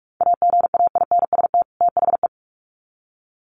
Morse code, heard as a keyed tone, about 46 wpm, 700 Hz. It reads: AGAINST THE
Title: Sets of 2 Words 45wpm